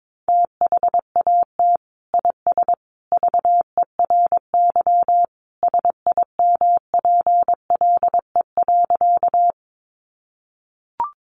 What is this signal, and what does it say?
Morse code, heard as a keyed tone, about 22 wpm, 700 Hz. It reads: THAT IS VERY SIMPLE.
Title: Sentences from Top 400 Words 22wpm